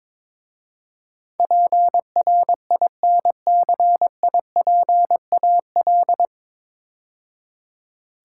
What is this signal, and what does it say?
Morse code, heard as a keyed tone, about 22 wpm, 700 Hz. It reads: PRINCIPAL
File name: 9 Letter Words - First 3 Letters Given - 22wpm